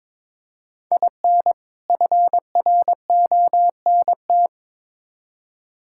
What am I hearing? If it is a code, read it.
Morse code, heard as a keyed tone, about 21 wpm, 700 Hz. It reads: IN FRONT
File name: Build A Phrase - 2-Word Phrases - 22wpm